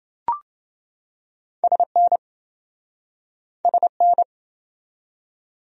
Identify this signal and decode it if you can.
Morse code, heard as a keyed tone, about 28 wpm, 700 Hz. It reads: SN SN